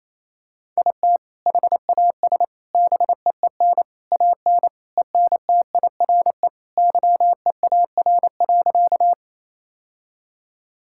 Morse code, heard as a keyed tone, about 28 wpm, 700 Hz. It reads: IT HAS BEEN AN ENTIRE YEAR.
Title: Sentences from Top 1000 Words - Review Part 5 - 28wpm